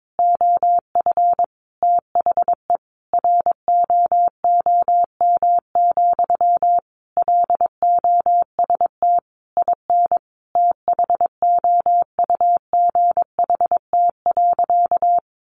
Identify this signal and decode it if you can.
Morse code, heard as a keyed tone, about 22 wpm, 700 Hz. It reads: OF THE ROOM, LOST IN THOUGHT.
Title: Sentences from Top 1400 Words 22wpm